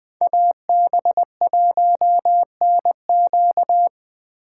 Morse code, heard as a keyed tone, about 25 wpm, 700 Hz. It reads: AB1NQ
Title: Speed Racing - USA 2x2 Call Signs 20wpm